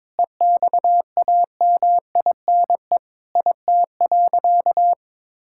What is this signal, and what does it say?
Morse code, heard as a keyed tone, about 22 wpm, 700 Hz. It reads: EXAMINE IT.
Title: Sentences from Top 1600 Words 22wpm